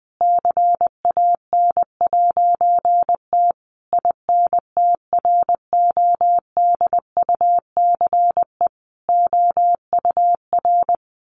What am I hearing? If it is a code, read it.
Morse code, heard as a keyed tone, about 20 wpm, 700 Hz. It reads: CAN'T INTRODUCE OUR
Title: A Sound Of Thunder 20wpm